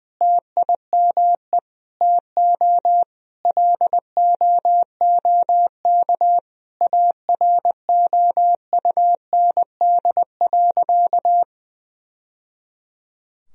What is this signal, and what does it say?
Morse code, heard as a keyed tone, about 21 wpm, 700 Hz. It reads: TIME TO LOOK AROUND.
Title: Sentences from Top 1000 Words - Review Part 3 - 20wpm